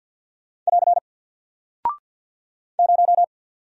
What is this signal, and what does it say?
Morse code, heard as a keyed tone, about 48 wpm, 700 Hz. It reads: <AR> 0
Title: Single Character (Letter:Number:Punctuation:Prosign) Alt 50wpm